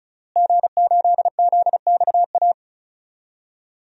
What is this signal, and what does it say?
Morse code, heard as a keyed tone, about 34 wpm, 700 Hz. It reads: G8ZXA
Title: DXCC-100 Call Signs - 5 Characters - Repeated 3x - 35wpm